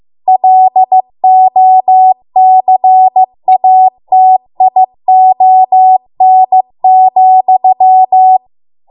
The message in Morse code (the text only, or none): LOCATION,